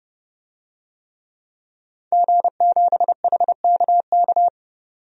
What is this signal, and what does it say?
Morse code, heard as a keyed tone, about 29 wpm, 700 Hz. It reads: G7HKK
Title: DXCC-100 Call Signs - 5 Characters - 30wpm